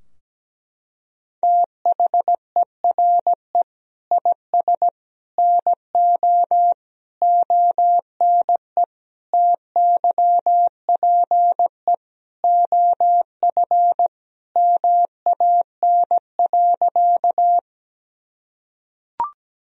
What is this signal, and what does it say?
Morse code, heard as a keyed tone, about 17 wpm, 700 Hz. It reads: THERE IS NO ONE TYPE OF MAN.
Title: Sentences from Top 1000 Words - Review Part 3 - 17wpm